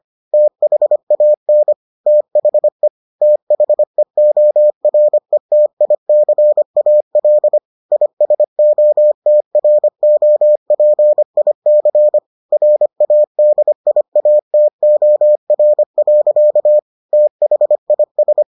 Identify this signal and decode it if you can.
Morse code, heard as a keyed tone, about 25 wpm, 600 Hz. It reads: THAN THE THEORETICAL ISOTROPIC RADIATOR. THIS